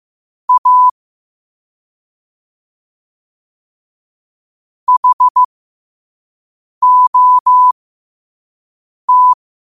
Morse code, heard as a keyed tone, about 15 wpm, spaced as for 5 wpm, 1000 Hz. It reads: A HOT
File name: qotdfiles / 5wpm